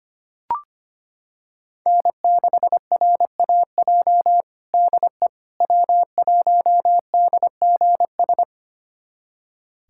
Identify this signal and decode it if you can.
Morse code, heard as a keyed tone, about 25 wpm, 700 Hz. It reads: N6RAJ DE W1DGS
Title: USA Callsign DE Callsign - 5 Characters - 25wpm